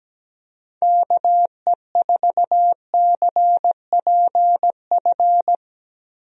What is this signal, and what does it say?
Morse code, heard as a keyed tone, about 17 wpm, 700 Hz. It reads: KE4CPF
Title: USA Call Signs - 6 Characters - Repeated 2x - 17wpm